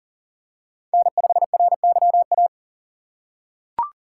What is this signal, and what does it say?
Morse code, heard as a keyed tone, about 40 wpm, 700 Hz. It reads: N5RYA
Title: USA Call Signs - 3x Character Spacing - 40wpm